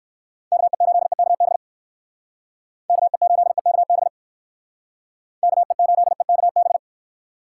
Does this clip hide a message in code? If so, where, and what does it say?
Yes: Morse code from 0.5 to 6.8 s, heard as a keyed tone, about 57 wpm, 700 Hz. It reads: KE8EKB KE8EKB KE8EKB